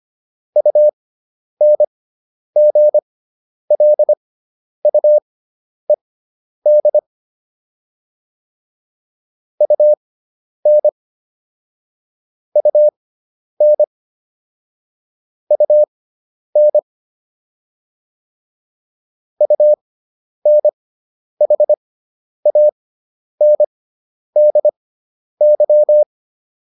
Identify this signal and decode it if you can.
Morse code, heard as a keyed tone, about 25 wpm, spaced as for 10 wpm, 600 Hz. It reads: UNGLUED UN UN UN UNHANDY